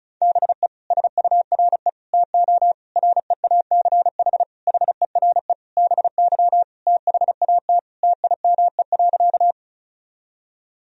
Morse code, heard as a keyed tone, about 36 wpm, 700 Hz. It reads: BE SURE TO REACH HERE BY THAT TIME.
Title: Sentences from Top 1000 Words - Review Part 2 - 35wpm